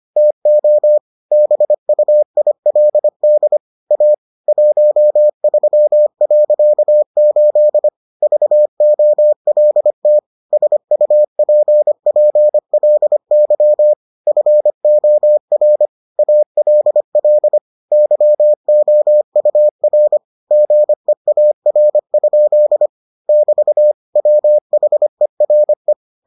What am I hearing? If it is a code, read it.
Morse code, heard as a keyed tone, about 25 wpm, 600 Hz. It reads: TO BUILD A 13.8 VOLT SUPPLY FOR ALL YOUR GEAR? <BT> WHERE